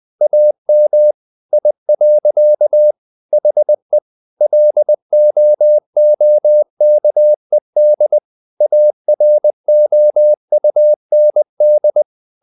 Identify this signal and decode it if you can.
Morse code, heard as a keyed tone, about 20 wpm, 600 Hz. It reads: AM I. HE LOOKED AROUND
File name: the old man and the sea / the old man and the sea part 1 20wpm